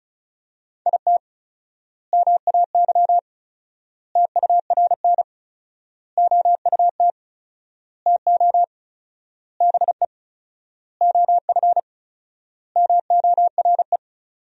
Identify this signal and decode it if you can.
Morse code, heard as a keyed tone, about 36 wpm, 700 Hz. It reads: IT MAY TURN OUT TO BE OF MORE
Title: 4x Word-Spacing - Sentences from Top 400 Words - 35wpm